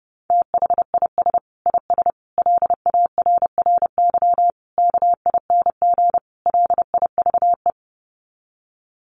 Morse code, heard as a keyed tone, about 30 wpm, 700 Hz. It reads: THIS IS LARRY KING LIVE
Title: Sets of 5 Words Encore2 30wpm